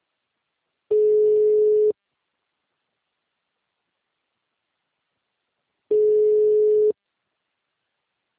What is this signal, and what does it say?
Ringback tone (Europe, 425 Hz), with speech.
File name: out- -2013- - - .1925